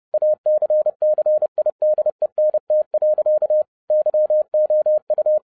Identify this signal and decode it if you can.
Morse code, heard as a keyed tone, about 30 wpm, 610 Hz. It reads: ACCIDENT. YOU
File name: 2011-07-mp3 / n60 BACK FROM A BIG ACCIDENT 30wpm